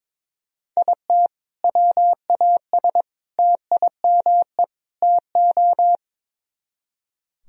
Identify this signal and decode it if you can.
Morse code, heard as a keyed tone, about 22 wpm, 700 Hz. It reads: IT WAS TIME TO
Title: Speed-Racing - Sets of 4 Words 22wpm